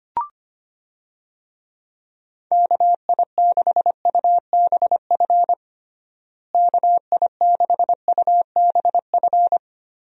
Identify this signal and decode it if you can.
Morse code, heard as a keyed tone, about 25 wpm, 700 Hz. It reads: KI6UBF KI6UBF
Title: USA Call Signs - 6 Characters - Repeated 2x - 25wpm